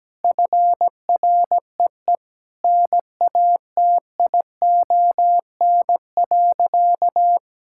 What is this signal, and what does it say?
Morse code, heard as a keyed tone, about 17 wpm, 700 Hz. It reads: FREE NATION.
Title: Sentences from Top 1000 Words - Review Part 6 - 17wpm